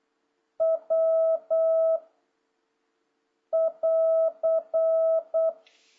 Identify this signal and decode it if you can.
Morse code, heard as a keyed tone, about 8 wpm, 650 Hz. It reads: W <AR>
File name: jh8cbh / 2026cw